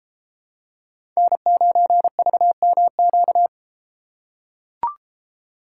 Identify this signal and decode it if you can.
Morse code, heard as a keyed tone, about 33 wpm, 700 Hz. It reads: N9VMQ